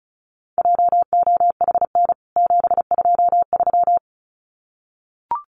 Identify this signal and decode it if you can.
Morse code, heard as a keyed tone, about 34 wpm, 700 Hz. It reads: JOHN 723
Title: CWT Exchange 35wpm